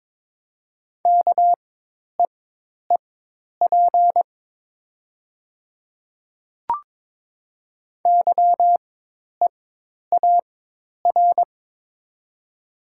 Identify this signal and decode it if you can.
Morse code, heard as a keyed tone, about 22 wpm, spaced as for 10 wpm, 700 Hz. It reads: KEEP YEAR
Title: 4x Character Spacing - Top 300 Words - 22wpm